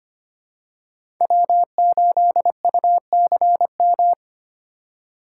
Morse code, heard as a keyed tone, about 25 wpm, 700 Hz. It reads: W8UCM
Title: USA Call Signs - 5 Characters - 2x Character Spacing - 25wpm